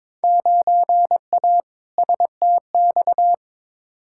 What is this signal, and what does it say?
Morse code, heard as a keyed tone, about 22 wpm, 700 Hz. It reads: 9A STX
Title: Field Day Exchanges - 22wpm